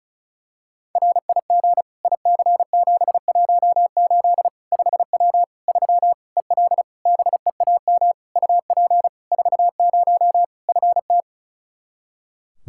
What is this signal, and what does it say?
Morse code, heard as a keyed tone, about 36 wpm, 700 Hz. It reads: RIG IC718 5W 3 EL BEAM UP 40 FT
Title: K1USN SST QSO Practice 35wpm